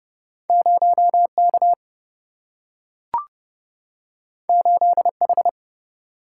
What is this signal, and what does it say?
Morse code, heard as a keyed tone, about 30 wpm, 700 Hz. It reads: 0K 8H
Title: Single-Letter Single-Number 30wpm